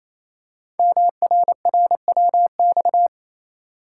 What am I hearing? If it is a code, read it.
Morse code, heard as a keyed tone, about 27 wpm, 700 Hz. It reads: MRRWX
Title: Code Groups - 5 Letters - 28wpm